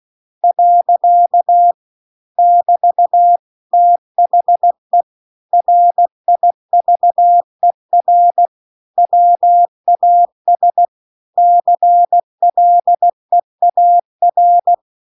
Morse code, heard as a keyed tone, about 16 wpm, 700 Hz. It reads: . <BT> THE RIVER WAS CLEAR
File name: CW-The-Big-Twoheaded-River-2-16WPM